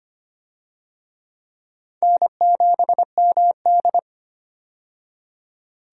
Morse code, heard as a keyed tone, about 25 wpm, 700 Hz. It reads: N7MD
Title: USA Call Signs - 4 Characters - 25wpm